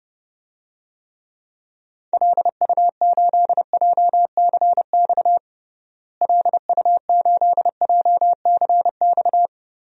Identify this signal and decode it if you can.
Morse code, heard as a keyed tone, about 29 wpm, 700 Hz. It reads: LU8JCX LU8JCX
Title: DXCC-100 Call Signs - 6 Characters - Repeated 3x - 30wpm